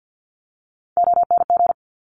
Morse code, heard as a keyed tone, about 50 wpm, 690 Hz. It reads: ONG
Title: Top Three Letter Combinations in Words 50wpm